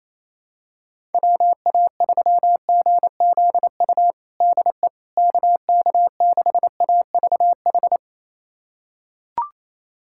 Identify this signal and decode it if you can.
Morse code, heard as a keyed tone, about 28 wpm, 700 Hz. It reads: WA3GZU DE KK6AVH
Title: USA Callsign DE Callsign - 6 Characters - 28wpm